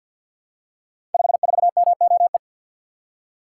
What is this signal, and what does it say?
Morse code, heard as a keyed tone, about 52 wpm, 700 Hz. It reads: 54MOE
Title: Code Groups - 5 Letter:Numbers - 50wpm